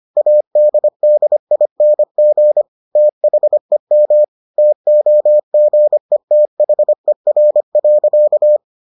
Morse code, heard as a keyed tone, about 25 wpm, 600 Hz. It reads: ADDING THEM TOGETHER.